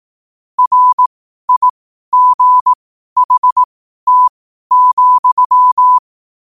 Morse code, heard as a keyed tone, about 18 wpm, spaced as for 12 wpm, 1000 Hz. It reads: RIGHT,
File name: qotdfiles / 12wpm